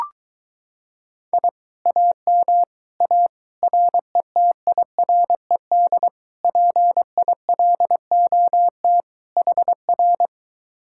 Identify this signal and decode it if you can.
Morse code, heard as a keyed tone, about 23 wpm, 700 Hz. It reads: I AM A RETIRED PILOT HR